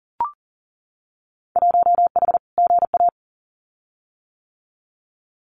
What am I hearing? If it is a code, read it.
Morse code, heard as a keyed tone, about 38 wpm, 700 Hz. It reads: 1H GA
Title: Winter Field Day Exchanges 40wpm